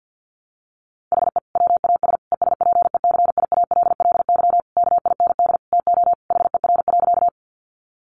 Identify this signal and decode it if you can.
Morse code, heard as a keyed tone, about 48 wpm, 700 Hz. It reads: HE WAS ESPECIALLY KIND TO HER.
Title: Sentences from Top 700 Words 50wpm